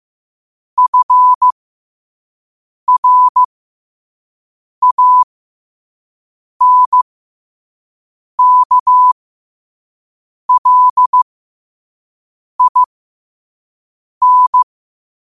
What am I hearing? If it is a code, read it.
Morse code, heard as a keyed tone, about 15 wpm, spaced as for 5 wpm, 1000 Hz. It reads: FRANKLIN
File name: qotdfiles / 5wpm